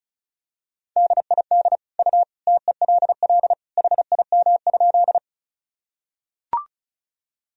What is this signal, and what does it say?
Morse code, heard as a keyed tone, about 35 wpm, 700 Hz. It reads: DID U TELL HIM?
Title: Sentences from Top 200 Words and Abbreviations 35wpm